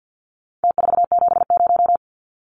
Morse code, heard as a keyed tone, about 50 wpm, 700 Hz. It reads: T470
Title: Numbers - Contest Style - 4-Digit 50wpm